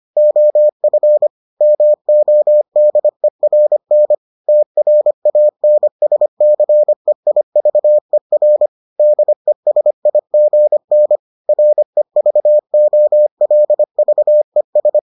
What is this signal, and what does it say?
Morse code, heard as a keyed tone, about 25 wpm, 600 Hz. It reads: OF MODERN TRANSCEIVER DESIGN REVOLVES